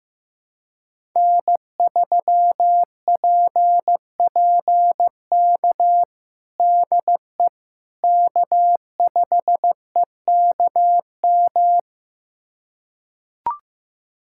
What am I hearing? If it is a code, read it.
Morse code, heard as a keyed tone, about 15 wpm, 700 Hz. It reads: N3PPK DE K5EKM
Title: USA Callsign DE Callsign - 5 Characters - 15wpm